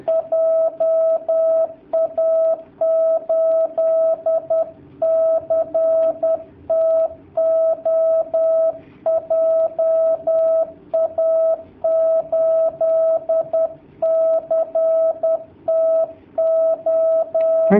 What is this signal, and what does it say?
Morse code, heard as a keyed tone, about 10 wpm, 650 Hz. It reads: JA8CTO JA8CTO
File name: jh8cbh / 2019cw / sp10- w